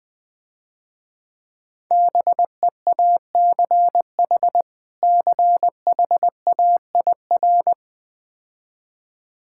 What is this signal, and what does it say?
Morse code, heard as a keyed tone, about 20 wpm, 700 Hz. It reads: BEACH CHAIR
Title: Difficult Words - Speed-Racing with Hints - 20wpm